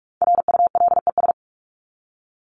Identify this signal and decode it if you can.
Morse code, heard as a keyed tone, about 44 wpm, 700 Hz. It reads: RULES
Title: ES:S Suffix - Top 100 Words with Base Words 45wpm